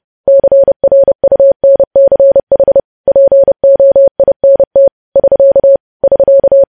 Morse code, heard as a keyed tone, about 30 wpm, 550 Hz. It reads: CRUNCH POINT <SK> <SK>
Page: news headlines in morse code at 30 wpm